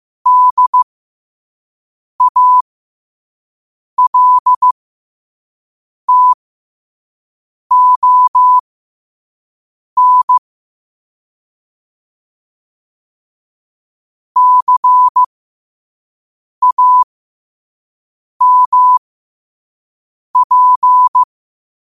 Morse code, heard as a keyed tone, about 15 wpm, spaced as for 5 wpm, 1000 Hz. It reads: DALTON CAMP
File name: qotdfiles / 5wpm